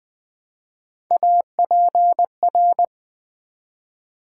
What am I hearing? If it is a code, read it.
Morse code, heard as a keyed tone, about 20 wpm, 700 Hz. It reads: APR
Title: Build A Word - 1 Letter At A Time - 5-Letter Words - 20wpm